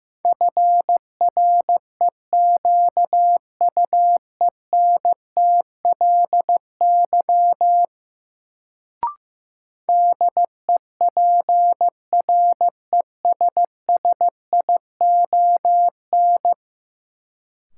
Morse code, heard as a keyed tone, about 15 wpm, 700 Hz. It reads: FREQUENTLY DEPRESSION
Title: Sentences from Top 1300 Words - New Word Drill Alt - 15wpm